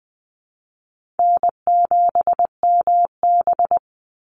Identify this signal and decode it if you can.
Morse code, heard as a keyed tone, about 20 wpm, 700 Hz. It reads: N7MB
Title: Call Signs - 20wpm